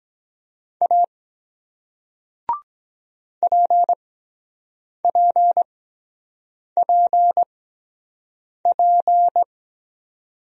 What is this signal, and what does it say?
Morse code, heard as a keyed tone, about 23 wpm, spaced as for 7 wpm, 700 Hz. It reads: A PPPP